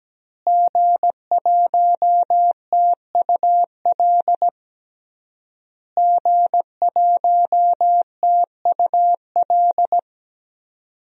Morse code, heard as a keyed tone, about 17 wpm, 700 Hz. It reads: G1TUL G1TUL
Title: DXCC-100 Call Signs - 5 Characters - Repeated 3x - 17wpm